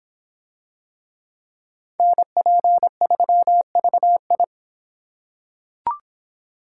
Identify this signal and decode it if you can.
Morse code, heard as a keyed tone, about 24 wpm, 700 Hz. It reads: NP3VI